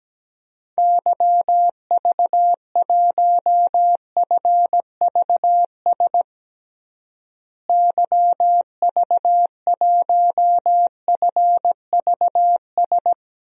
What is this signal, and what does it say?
Morse code, heard as a keyed tone, about 17 wpm, 700 Hz. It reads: YV1FVS YV1FVS
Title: DXCC-100 Call Signs - 6 Characters - Repeated 2x - 17wpm